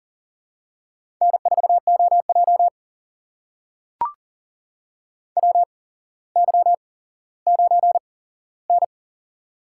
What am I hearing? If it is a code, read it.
Morse code, heard as a keyed tone, about 40 wpm, spaced as for 17 wpm, 700 Hz. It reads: N4OJ WY9N
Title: USA Call Signs - 4 Characters - 8x Character Spacing - 40wpm